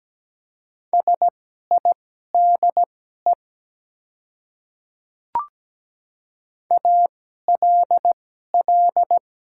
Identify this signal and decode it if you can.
Morse code, heard as a keyed tone, about 17 wpm, spaced as for 12 wpm, 690 Hz. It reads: SIDE ALL
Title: 2x Character Spacing - Top 100 Words - 17wpm